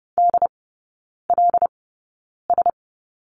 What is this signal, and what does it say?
Morse code, heard as a keyed tone, about 30 wpm, spaced as for 9 wpm, 700 Hz. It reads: DLS
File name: Lesson 19e - Letter w - Three Characters 30wpm